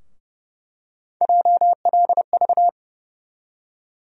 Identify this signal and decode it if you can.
Morse code, heard as a keyed tone, about 32 wpm, 700 Hz. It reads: JLV
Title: USA Call Sign Postfixes - 3 Letters 30wpm